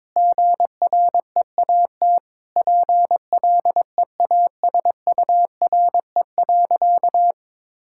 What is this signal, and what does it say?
Morse code, heard as a keyed tone, about 22 wpm, 700 Hz. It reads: GREAT PLEASURE.
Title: Sentences from Top 1600 Words 22wpm